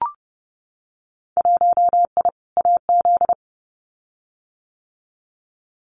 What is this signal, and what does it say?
Morse code, heard as a keyed tone, about 30 wpm, 700 Hz. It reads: 1I AZ
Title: Winter Field Day Exchanges 30wpm